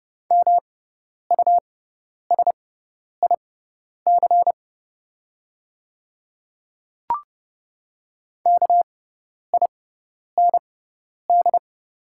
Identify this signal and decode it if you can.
Morse code, heard as a keyed tone, about 31 wpm, spaced as for 10 wpm, 700 Hz. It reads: MUSIC KIND
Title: 6x Character Spacing - Top 500 Words - 30wpm